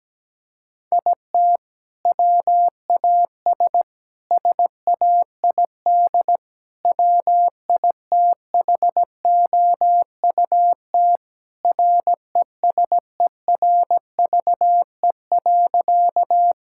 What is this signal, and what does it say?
Morse code, heard as a keyed tone, about 17 wpm, 700 Hz. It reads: IT WAS SAID WITHOUT RESERVE.
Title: Sentences from Top 1000 Words - Review Part 6 - 17wpm